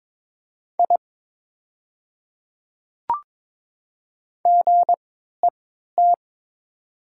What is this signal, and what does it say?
Morse code, heard as a keyed tone, about 22 wpm, spaced as for 12 wpm, 700 Hz. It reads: I GET
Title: 3x Character Spacing - Top 300 Words - 22wpm